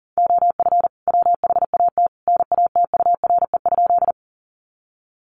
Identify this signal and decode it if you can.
Morse code, heard as a keyed tone, about 40 wpm, 700 Hz. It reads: OF WHAT NATURE?
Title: Sentences from Top 600 Words 40wpm